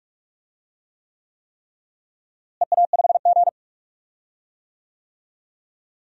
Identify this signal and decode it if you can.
Morse code, heard as a keyed tone, about 44 wpm, 700 Hz. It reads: EA5G
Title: DXCC-100 Call Signs - 4 Characters - Repeated 3x - 45wpm